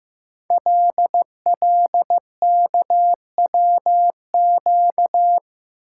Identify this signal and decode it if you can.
Morse code, heard as a keyed tone, about 15 wpm, 700 Hz. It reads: LLKWQ
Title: Code Groups - 5 Letter:Numbers - 15wpm